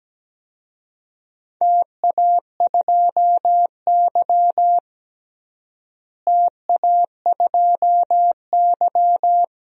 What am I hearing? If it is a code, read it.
Morse code, heard as a keyed tone, about 17 wpm, 700 Hz. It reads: TA2Y TA2Y
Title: DXCC-100 Call Signs - 4 Characters - Repeated 2x - 17wpm